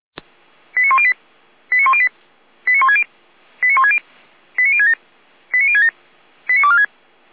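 Five-tone selective call (ZVEI), heard as quick runs of five tones, tones of about 70 ms. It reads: 89108 89108 89160 89160 89067 89067 89246
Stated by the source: Five-tone ZVEI selective call received on an SDR.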